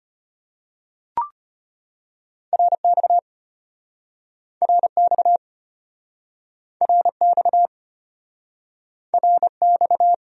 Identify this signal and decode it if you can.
Morse code, heard as a keyed tone, about 33 wpm, 700 Hz. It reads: RX RX RX RX